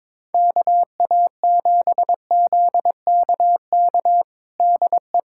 Morse code, heard as a keyed tone, about 22 wpm, 700 Hz. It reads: KA7ZKK DE
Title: USA Callsign DE Callsign - 6 Characters - 22wpm